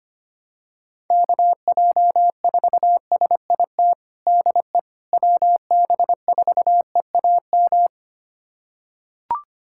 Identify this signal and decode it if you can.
Morse code, heard as a keyed tone, about 25 wpm, 700 Hz. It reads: KJ4SIT DE WB4EAM
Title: USA Callsign DE Callsign - 6 Characters - 25wpm